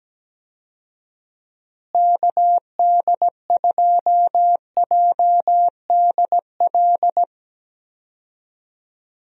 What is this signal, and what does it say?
Morse code, heard as a keyed tone, about 17 wpm, 700 Hz. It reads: KD2JDL
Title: USA Call Signs - 6 Characters - Repeated 3x - 17wpm